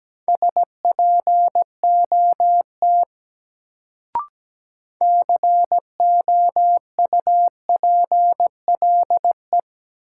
Morse code, heard as a keyed tone, about 17 wpm, 700 Hz. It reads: SPOT COUPLE
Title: Sentences from Top 1100 Words - New Word Drill Alt - 17wpm